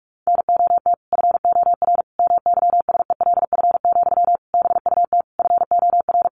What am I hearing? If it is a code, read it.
Morse code, heard as a keyed tone, about 46 wpm, 700 Hz. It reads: NOT FOR MYSELF, BUT FOR
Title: Sentences from Top 1300 Words 45wpm